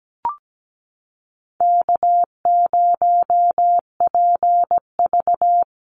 Morse code, heard as a keyed tone, about 17 wpm, 700 Hz. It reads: K0PV
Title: Call Signs Repeated 2x - 17wpm